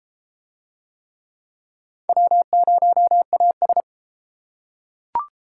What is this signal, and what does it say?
Morse code, heard as a keyed tone, about 28 wpm, 700 Hz. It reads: W0AS